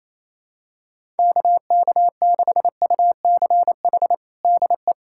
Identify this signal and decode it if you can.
Morse code, heard as a keyed tone, about 28 wpm, 700 Hz. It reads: KK6UCH DE
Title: USA Callsign DE Callsign 28wpm